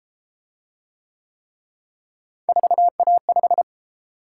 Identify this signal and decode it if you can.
Morse code, heard as a keyed tone, about 31 wpm, 700 Hz. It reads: 4A5